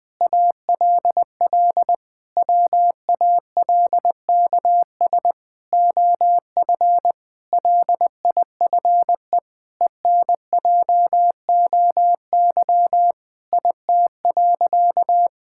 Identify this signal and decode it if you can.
Morse code, heard as a keyed tone, about 20 wpm, 700 Hz. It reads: ALL WALKS OF LIFE ENJOY IT.
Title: Sentences with 6-Word Phrases - Spoken Beginning - 20wpm